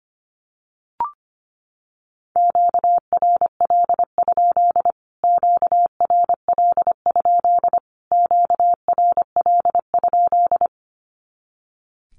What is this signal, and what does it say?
Morse code, heard as a keyed tone, about 25 wpm, 700 Hz. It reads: QRL? QRL? QRL?
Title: Q-Codes Repeated 3x 25wpm